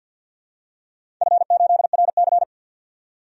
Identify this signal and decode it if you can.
Morse code, heard as a keyed tone, about 53 wpm, 700 Hz. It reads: F8RC
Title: DXCC-100 Call Signs - 4 Characters - 50wpm